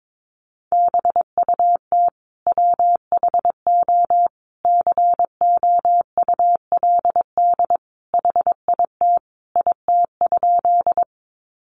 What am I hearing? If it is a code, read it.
Morse code, heard as a keyed tone, about 22 wpm, 700 Hz. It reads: BUT WHO COULD HIT IT?
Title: Sentences from Top 1200 Words 22wpm